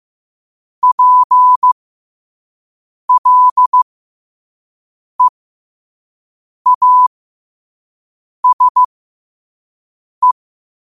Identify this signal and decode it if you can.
Morse code, heard as a keyed tone, about 15 wpm, spaced as for 5 wpm, 1000 Hz. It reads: PLEASE
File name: qotdfiles / 5wpm